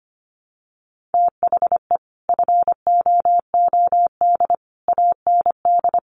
Morse code, heard as a keyed tone, about 25 wpm, 700 Hz. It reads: THE FOOD AND
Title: Sets of 5 Words 25wpm